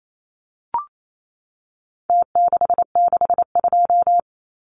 Morse code, heard as a keyed tone, about 28 wpm, 700 Hz. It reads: T662
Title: Numbers - Contest Style - 4-Digit 28wpm